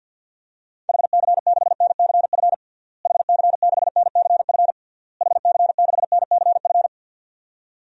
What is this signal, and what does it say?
Morse code, heard as a keyed tone, about 52 wpm, 700 Hz. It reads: HC6NCF HC6NCF HC6NCF
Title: DXCC-100 Call Signs - 6 Characters - Repeated 3x - 50wpm